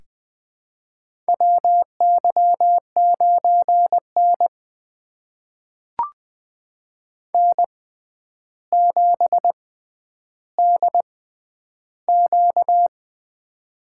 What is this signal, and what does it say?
Morse code, heard as a keyed tone, about 20 wpm, spaced as for 10 wpm, 700 Hz. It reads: WY9N N7DQ
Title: USA Call Signs - 4 Characters - 6x Character Spacing - 20wpm